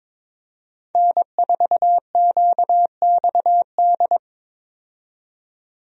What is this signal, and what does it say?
Morse code, heard as a keyed tone, about 21 wpm, 700 Hz. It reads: N4QXD
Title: USA Call Signs - 5 Characters - 22wpm